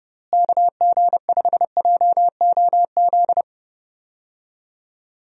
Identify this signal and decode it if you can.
Morse code, heard as a keyed tone, about 30 wpm, 700 Hz. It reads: KG5JOZ
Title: USA Call Signs - 6 Characters - Repeated 2x - 30wpm